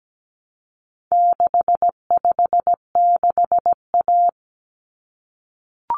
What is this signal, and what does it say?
Morse code, heard as a keyed tone, about 17 wpm, 700 Hz. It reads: 656A
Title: Common Cut Numbers - 4-Digit 17wpm